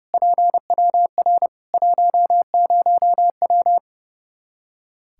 Morse code, heard as a keyed tone, about 30 wpm, 700 Hz. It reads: PWR 10W
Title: Practice QSO - Exchange 2 - Info - 30wpm